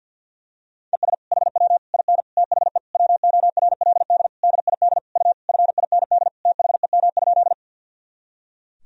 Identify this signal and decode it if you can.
Morse code, heard as a keyed tone, about 50 wpm, 700 Hz. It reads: ES HW IN THE WORLD DID U FIND THEM?
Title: Sentences from Top 200 Words and Abbreviations 50wpm